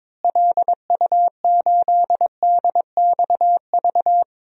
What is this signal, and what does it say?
Morse code, heard as a keyed tone, about 22 wpm, 700 Hz. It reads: LU8DXV
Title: DXCC-100 Call Signs - 6 Characters - Repeated 2x - 22wpm